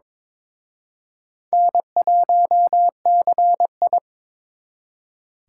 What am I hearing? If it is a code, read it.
Morse code, heard as a keyed tone, about 26 wpm, 700 Hz. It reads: N1CI
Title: Speed Racing - USA 1x2 Call Signs 22wpm